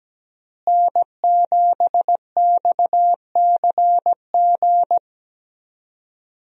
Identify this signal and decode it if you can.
Morse code, heard as a keyed tone, about 17 wpm, 700 Hz. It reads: N7XCG
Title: USA Call Signs - 5 Characters - Repeated 3x - 17wpm